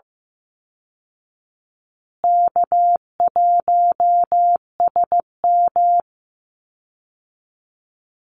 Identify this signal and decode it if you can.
Morse code, heard as a keyed tone, about 15 wpm, 700 Hz. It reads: K1SM
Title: Easy Call Signs Repeated 2x - 15wpm